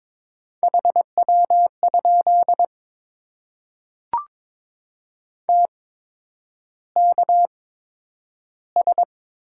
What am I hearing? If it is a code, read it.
Morse code, heard as a keyed tone, about 22 wpm, spaced as for 9 wpm, 700 Hz. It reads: HW? TKS
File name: Top 90 QSO Elements - 8x Character Spacing - 22wpm